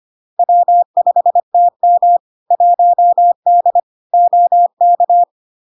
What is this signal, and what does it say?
Morse code, heard as a keyed tone, about 25 wpm, 700 Hz. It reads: W5TM 1D OK
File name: Log-in-CW-25wpm-700-Hz